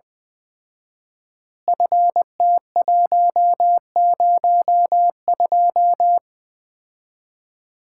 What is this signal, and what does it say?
Morse code, heard as a keyed tone, about 20 wpm, 700 Hz. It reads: FT102
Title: Common QSO Phrases - 4x Word-Spacing - 20wpm